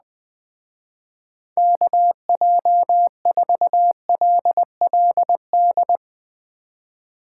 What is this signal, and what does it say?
Morse code, heard as a keyed tone, about 20 wpm, 700 Hz. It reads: KJ4LLD
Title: Speed Racing - USA Call Signs 17wpm